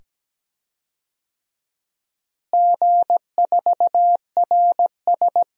Morse code, heard as a keyed tone, about 17 wpm, 700 Hz. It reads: G4RS
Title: DXCC-100 Call Signs - 4 Characters - Repeated 2x - 17wpm